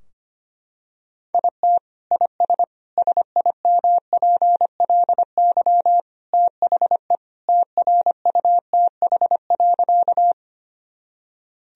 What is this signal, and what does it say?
Morse code, heard as a keyed tone, about 25 wpm, 700 Hz. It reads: IT IS SIMPLY THE TRUTH.
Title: Sentences from Top 1000 Words - Review Part 4 - 25wpm